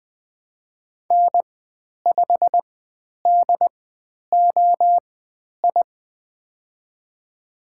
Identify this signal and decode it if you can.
Morse code, heard as a keyed tone, about 20 wpm, spaced as for 10 wpm, 700 Hz. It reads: N5DOI
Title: 095 Lesson 16g - Letter c - Call Signs 20wpm